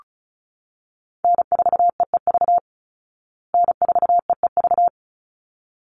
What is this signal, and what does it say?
Morse code, heard as a keyed tone, about 34 wpm, 700 Hz. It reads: N4EEV N4EEV
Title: Call Signs Repeated 3x - 35wpm